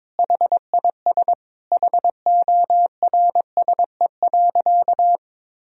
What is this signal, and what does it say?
Morse code, heard as a keyed tone, about 22 wpm, 700 Hz. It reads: HIS HORSE.
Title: Sentences from Top 1000 Words - Review Part 3 - 22wpm